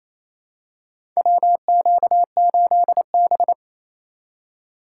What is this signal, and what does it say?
Morse code, heard as a keyed tone, about 28 wpm, 700 Hz. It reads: WQ8B
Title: USA Call Signs - 2x Character Spacing - 28wpm